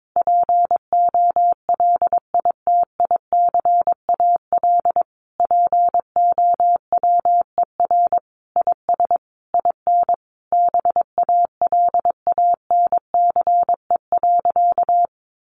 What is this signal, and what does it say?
Morse code, heard as a keyed tone, about 22 wpm, 700 Hz. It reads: POLITICAL POWER IS IN BALANCE.
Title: Sentences from Top 900 Words 22wpm